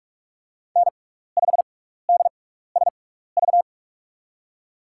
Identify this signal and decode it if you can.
Morse code, heard as a keyed tone, about 44 wpm, spaced as for 15 wpm, 690 Hz. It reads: N5DSV
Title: USA Call Signs - 5 Characters - 6x Character Spacing - 45wpm